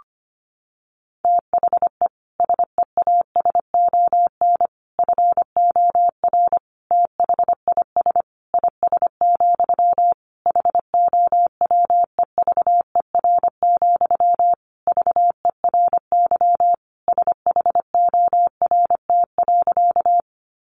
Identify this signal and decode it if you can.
Morse code, heard as a keyed tone, about 25 wpm, 700 Hz. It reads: THE SEASON FOR THIS IS, HOWEVER, VERY SHORT.
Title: Sentences from Top 1400 Words 25wpm